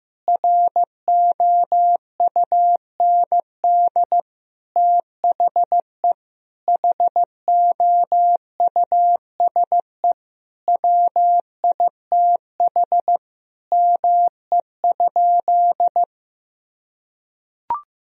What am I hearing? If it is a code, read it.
Morse code, heard as a keyed tone, about 15 wpm, 700 Hz. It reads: ROUND THE HOUSE WITH ME?
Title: Sentences from Top 1000 Words - Review Part 1 - 15wpm